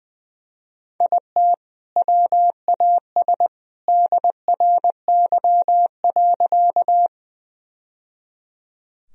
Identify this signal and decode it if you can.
Morse code, heard as a keyed tone, about 20 wpm, 700 Hz. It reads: IT WAS DRY.
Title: Sentences from Top 1000 Words - Review Part 3 - 20wpm